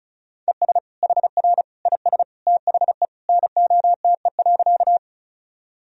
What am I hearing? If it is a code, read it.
Morse code, heard as a keyed tone, about 36 wpm, 700 Hz. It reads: ES HR IS THE NOTE.
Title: Sentences from Top 500 Words and Abbreviations 35wpm